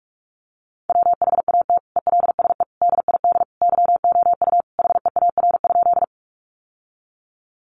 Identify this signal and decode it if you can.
Morse code, heard as a keyed tone, about 44 wpm, 700 Hz. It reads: WHAT ELSE DID YOU HEAR?
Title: Sentences from Top 800 Words 45wpm